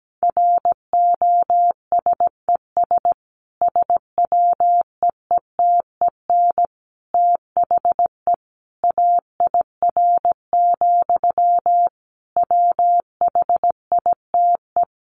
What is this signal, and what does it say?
Morse code, heard as a keyed tone, about 17 wpm, 700 Hz. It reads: ROSES SWEETEN THE AIR, WHITE
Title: A Sound Of Thunder 17wpm